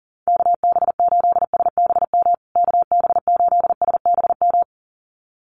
Morse code, heard as a keyed tone, about 40 wpm, 700 Hz. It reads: KB8SBM KB8SBM
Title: USA Call Signs Repeated 2x 40wpm